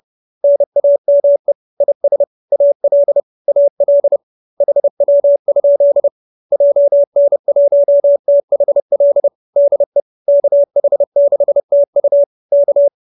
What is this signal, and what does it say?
Morse code, heard as a keyed tone, about 30 wpm, 580 Hz. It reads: NAME IS AL AL HW? JN1THL DE KH6TU K
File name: QSO309 30